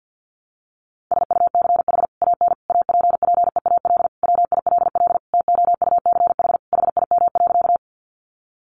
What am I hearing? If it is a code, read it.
Morse code, heard as a keyed tone, about 50 wpm, 700 Hz. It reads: SUCH AN APPEAL WILL TOUCH HIM.
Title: Sentences from Top 1300 Words 50wpm